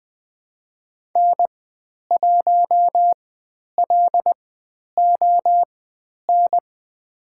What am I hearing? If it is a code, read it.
Morse code, heard as a keyed tone, about 20 wpm, spaced as for 10 wpm, 700 Hz. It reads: N1LON
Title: 095 Lesson 16g - Letter c - Call Signs 20wpm